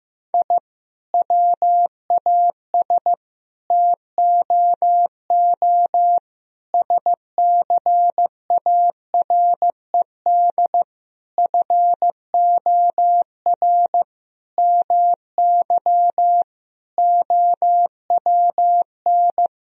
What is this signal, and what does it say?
Morse code, heard as a keyed tone, about 15 wpm, 700 Hz. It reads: I WAS TOO SCARED FOR MY OWN
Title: Sentences from Top 1000 Words - Review Part 4 - 15wpm